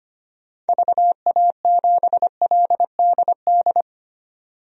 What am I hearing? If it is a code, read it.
Morse code, heard as a keyed tone, about 25 wpm, 690 Hz. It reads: VA7LDD
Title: DXCC-100 Call Signs - 6 Characters - 25wpm